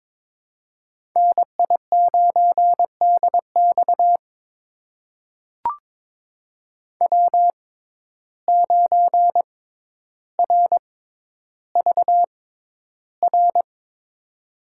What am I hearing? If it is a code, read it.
Morse code, heard as a keyed tone, about 22 wpm, spaced as for 11 wpm, 700 Hz. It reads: NI9DX W9RVR
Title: USA Call Signs - 5 Characters - 6x Character Spacing - 22wpm